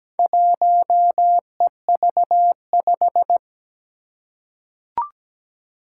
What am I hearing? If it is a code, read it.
Morse code, heard as a keyed tone, about 17 wpm, 700 Hz. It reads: 1EV5
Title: Four Challenging Letter-Number Combinations 17wpm